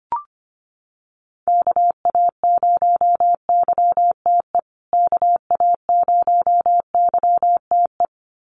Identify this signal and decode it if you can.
Morse code, heard as a keyed tone, about 25 wpm, 700 Hz. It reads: KA0YTE KA0YTE
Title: USA Call Signs Repeated 2x 25wpm